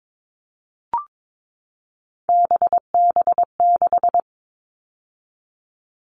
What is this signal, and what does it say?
Morse code, heard as a keyed tone, about 22 wpm, 700 Hz. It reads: BB6
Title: All Cut Numbers - 3-Digit 22wpm